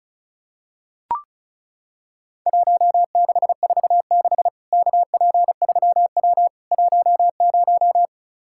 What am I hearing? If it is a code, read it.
Morse code, heard as a keyed tone, about 34 wpm, 700 Hz. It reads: 1646 KP3W 10
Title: Worked All Europe DX Contest - QTC Exchange - 35wpm